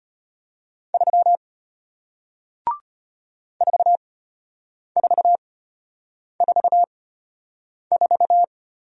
Morse code, heard as a keyed tone, about 34 wpm, spaced as for 8 wpm, 700 Hz. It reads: 3 4444